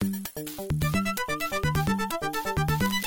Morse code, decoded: none